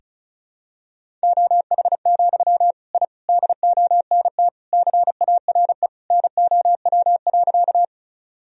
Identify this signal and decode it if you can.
Morse code, heard as a keyed tone, about 34 wpm, 700 Hz. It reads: OH, I DONT CARE NOW.
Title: Sentences from Top 1000 Words - Review Part 1 - 35wpm